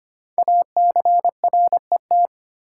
Morse code, heard as a keyed tone, about 25 wpm, 700 Hz. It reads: ACRET
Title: Code Groups - 5 Letters - 25wpm